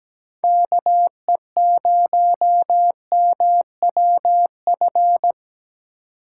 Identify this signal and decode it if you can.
Morse code, heard as a keyed tone, about 17 wpm, 700 Hz. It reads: KE0MWF
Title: USA Call Signs - 2x Character Spacing - 17wpm